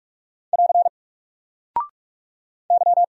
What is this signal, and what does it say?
Morse code, heard as a keyed tone, about 44 wpm, 700 Hz. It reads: <AR> Y
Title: Single Character (Letter:Number:Punctuation:Prosign) 45wpm